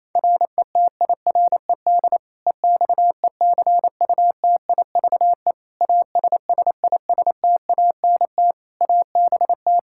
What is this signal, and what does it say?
Morse code, heard as a keyed tone, about 28 wpm, 700 Hz. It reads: RETIRED EXECUTIVE ASSISTANT ABT
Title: Practice QSO - Exchange 3 - More Info - 28wpm